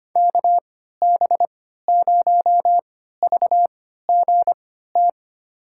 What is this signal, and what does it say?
Morse code, heard as a keyed tone, about 25 wpm, spaced as for 14 wpm, 700 Hz. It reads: KB0VGT
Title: USA Call Signs - 6 Characters - 3x Character Spacing - 25wpm